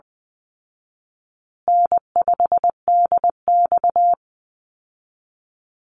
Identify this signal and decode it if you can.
Morse code, heard as a keyed tone, about 19 wpm, 700 Hz. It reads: N5DX
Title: Easy Call Signs Repeated 2x - 20wpm